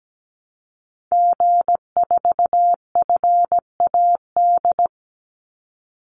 Morse code, heard as a keyed tone, about 17 wpm, 700 Hz. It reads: G4FAD
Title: Call Signs - 17wpm